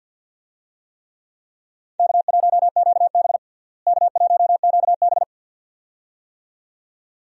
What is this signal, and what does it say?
Morse code, heard as a keyed tone, about 50 wpm, 700 Hz. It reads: K1QB K1QB
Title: USA Call Signs - 4 Characters - Repeated 2x - 50wpm